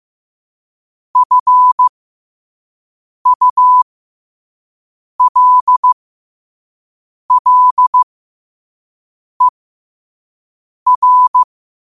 Morse code, heard as a keyed tone, about 15 wpm, spaced as for 5 wpm, 1000 Hz. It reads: FULLER
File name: qotdfiles / 5wpm